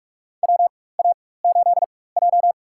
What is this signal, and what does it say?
Morse code, heard as a keyed tone, about 46 wpm, spaced as for 21 wpm, 700 Hz. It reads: WA8J
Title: USA Call Signs - 4 Characters - 4x Character Spacing - 45wpm